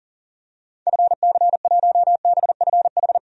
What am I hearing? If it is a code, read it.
Morse code, heard as a keyed tone, about 40 wpm, 700 Hz. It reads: FC1BFH
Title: DXCC-100 Call Signs - 6 Characters - 40wpm